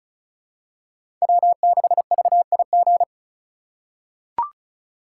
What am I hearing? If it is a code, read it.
Morse code, heard as a keyed tone, about 36 wpm, 700 Hz. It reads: W6VIG
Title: USA Call Signs - 5 Characters - 2x Character Spacing - 35wpm